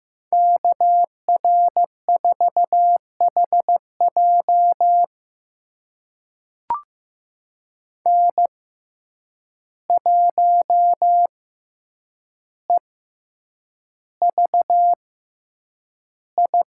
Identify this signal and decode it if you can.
Morse code, heard as a keyed tone, about 15 wpm, spaced as for 8 wpm, 700 Hz. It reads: KR4HJ N1EVI
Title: USA Call Signs - 5 Characters - 6x Character Spacing - 15wpm